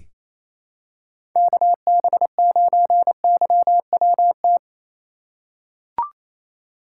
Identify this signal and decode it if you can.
Morse code, heard as a keyed tone, about 28 wpm, 700 Hz. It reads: KB9YWT
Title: USA Call Signs - 4x Character Spacing - 28wpm